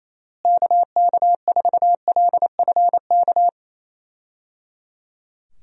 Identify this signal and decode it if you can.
Morse code, heard as a keyed tone, about 27 wpm, 700 Hz. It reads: KK4LFK
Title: USA Call Signs - 6 Characters - Repeated 3x - 28wpm